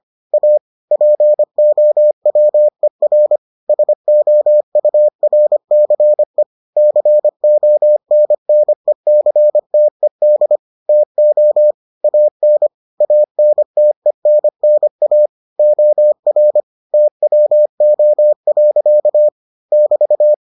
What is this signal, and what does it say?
Morse code, heard as a keyed tone, about 25 wpm, 600 Hz. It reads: A POWER SOURCE CONNECTED TO AN ANTENNA OR TWO. <BT>